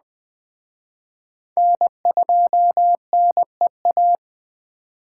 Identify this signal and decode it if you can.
Morse code, heard as a keyed tone, about 21 wpm, 700 Hz. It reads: N2NEA
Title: Speed Racing - USA 1x3 Call Signs 17wpm